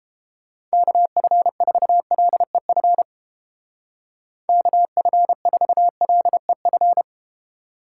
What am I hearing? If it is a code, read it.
Morse code, heard as a keyed tone, about 30 wpm, 700 Hz. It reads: KF4LEF KF4LEF